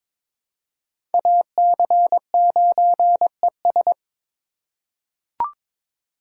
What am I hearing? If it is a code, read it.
Morse code, heard as a keyed tone, about 22 wpm, 700 Hz. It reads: AC9ES
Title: USA Call Signs - 4x Character Spacing - 22wpm